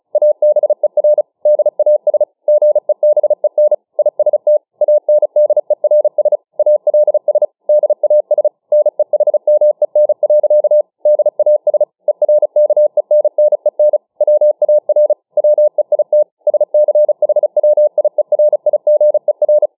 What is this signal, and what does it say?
Morse code, heard as a keyed tone, about 36 wpm, 600 Hz. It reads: ABER DAS GEBEN IST ANDERS ALS DAS NEHMEN. DAS ERKENNEN WAR WEIT SCHWIERIGER
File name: dk5ke / Morsemogeln 35WpM